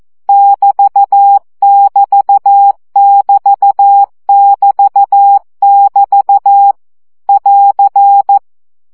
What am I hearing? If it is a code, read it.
Morse code, heard as a keyed tone, about 14 wpm, 800 Hz. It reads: <BT><BT><BT><BT><BT> <AR>
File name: CW gyakorlas egyenlo 12 baud